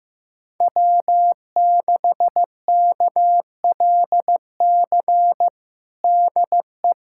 Morse code, heard as a keyed tone, about 15 wpm, 700 Hz. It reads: W6KLC DE
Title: USA Callsign DE Callsign - 5 Characters - 15wpm